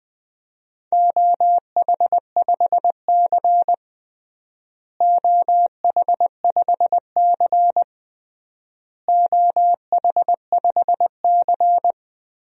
Morse code, heard as a keyed tone, about 20 wpm, 700 Hz. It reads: OH5C OH5C OH5C
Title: DXCC-100 Call Signs - 4 Characters - Repeated 3x - 20wpm